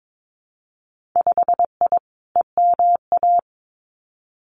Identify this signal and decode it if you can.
Morse code, heard as a keyed tone, about 22 wpm, 700 Hz. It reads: 5I EMA
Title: Winter Field Day Exchanges Encore 22wpm